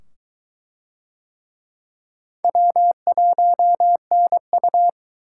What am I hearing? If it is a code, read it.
Morse code, heard as a keyed tone, about 22 wpm, 700 Hz. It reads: W1NU